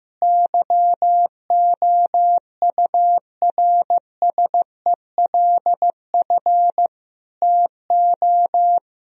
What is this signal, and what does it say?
Morse code, heard as a keyed tone, about 15 wpm, 700 Hz. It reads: YOURSELF TO
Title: Sentences from Top 1000 Words - Review Part 4 - 15wpm